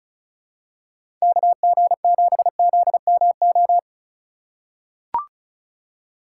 Morse code, heard as a keyed tone, about 34 wpm, 700 Hz. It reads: KG7ZMO
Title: USA Call Signs - 2x Character Spacing - 35wpm